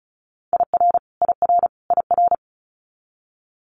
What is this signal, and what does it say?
Morse code, heard as a keyed tone, about 36 wpm, 700 Hz. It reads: IR IR IR
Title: Multi-letter Phonemes Repeated 3x 35wpm